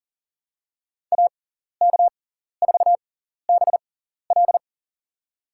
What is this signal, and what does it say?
Morse code, heard as a keyed tone, about 40 wpm, spaced as for 14 wpm, 700 Hz. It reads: AK4BL
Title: USA Call Signs - 5 Characters - 6x Character Spacing - 40wpm